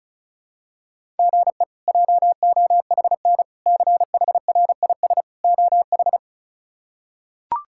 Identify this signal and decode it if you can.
Morse code, heard as a keyed tone, about 36 wpm, 700 Hz. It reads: GE JOHN CHRIS OH
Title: K1USN SST QSO Practice - 35wpm